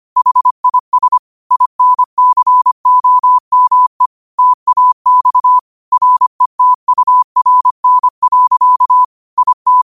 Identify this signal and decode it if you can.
Morse code, heard as a keyed tone, about 25 wpm, 1000 Hz. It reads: HIS INCOME TAX RETURN. IT